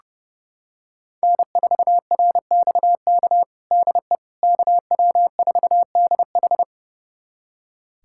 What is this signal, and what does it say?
Morse code, heard as a keyed tone, about 30 wpm, 700 Hz. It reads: N4RXK DE KW4DH
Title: USA Callsign DE Callsign - 5 Characters - 30wpm